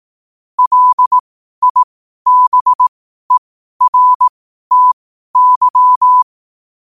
Morse code, heard as a keyed tone, about 18 wpm, spaced as for 12 wpm, 1000 Hz. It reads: LIBERTY
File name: qotdfiles / 12wpm